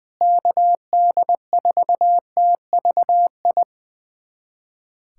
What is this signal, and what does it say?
Morse code, heard as a keyed tone, about 21 wpm, 700 Hz. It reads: KD4TVI
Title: USA Callsign DE Callsign 20wpm